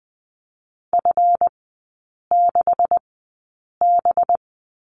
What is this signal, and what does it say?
Morse code, heard as a keyed tone, about 20 wpm, spaced as for 8 wpm, 690 Hz. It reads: F6B
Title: 212 Lesson 34e - Number 8 - Three Characters 20wpm